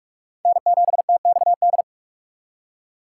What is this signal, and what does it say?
Morse code, heard as a keyed tone, about 38 wpm, 700 Hz. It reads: N7TXD